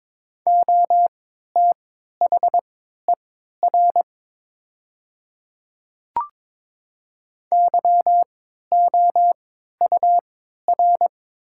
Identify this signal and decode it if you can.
Morse code, heard as a keyed tone, about 22 wpm, spaced as for 12 wpm, 700 Hz. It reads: OTHER YOUR
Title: 3x Character Spacing - Top 100 Words - 22wpm